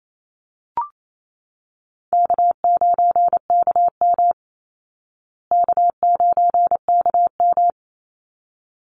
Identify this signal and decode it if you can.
Morse code, heard as a keyed tone, about 28 wpm, 700 Hz. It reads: K9KM K9KM
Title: Call Signs Repeated 3x - 28wpm